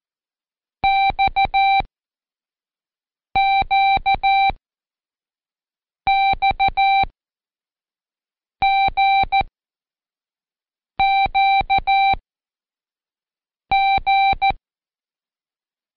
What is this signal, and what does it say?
Morse code, heard as a keyed tone, about 14 wpm, 780 Hz. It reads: X Q X G Q G
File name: Code Quick / AudioDownld1 / Seg54-Day6-Pract GQX